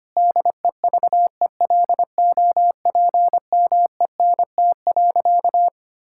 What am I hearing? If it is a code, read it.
Morse code, heard as a keyed tone, about 25 wpm, 700 Hz. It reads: DEVELOPMENT.
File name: practice / 25wpm / Difficult Sentences with Hints - 25wpm